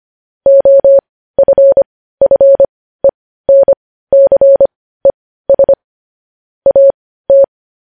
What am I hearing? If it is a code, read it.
Morse code, heard as a keyed tone, about 25 wpm, spaced as for 15 wpm, 550 Hz. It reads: OFFENCES AT
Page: news-headlines-in-morse-code-at-15-wpm